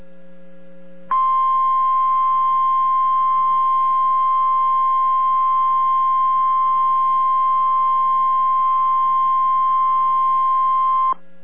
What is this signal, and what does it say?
Emergency Alert System (SAME) tones (the 1050 Hz attention tone). It is heard from 1.1 to 11.2 s.